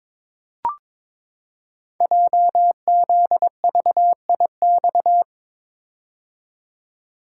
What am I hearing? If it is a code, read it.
Morse code, heard as a keyed tone, about 22 wpm, 700 Hz. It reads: JZVIX
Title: Code Groups - 5 Letters - 22wpm